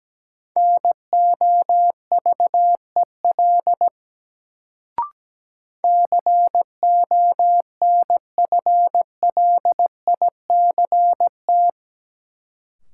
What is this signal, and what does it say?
Morse code, heard as a keyed tone, about 17 wpm, 700 Hz. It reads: NOVEL CONFLICT
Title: Sentences from Top 1300 Words - New Word Drill - 17wpm